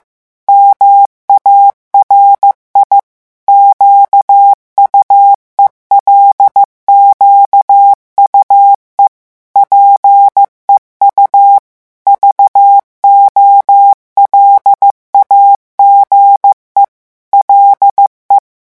Morse code, heard as a keyed tone, about 15 wpm, 800 Hz. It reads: MARI QUELQUE PEU VOLAGE LE
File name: morse mp3 / revision9 citations humoristiques vitesse 14 mots minute